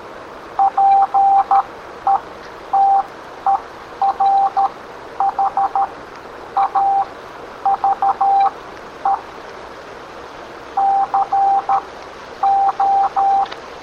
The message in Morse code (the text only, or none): PETER HAVE CO